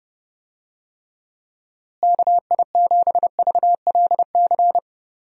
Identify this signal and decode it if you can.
Morse code, heard as a keyed tone, about 30 wpm, 700 Hz. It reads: KI7VLC
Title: USA Call Signs - 6 Characters - Repeated 3x - 30wpm